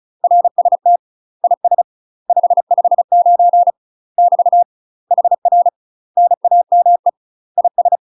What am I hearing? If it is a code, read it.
Morse code, heard as a keyed tone, about 34 wpm, 700 Hz. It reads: RST IS 559 <BT> HR NAME IS
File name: UK MASTER QSOs TRAINING / 35WPM / UK MASTER QSOs 0043